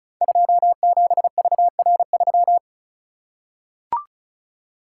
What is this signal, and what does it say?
Morse code, heard as a keyed tone, about 36 wpm, 700 Hz. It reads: 27VR3
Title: Code Groups - 5 Letter:Numbers - 35wpm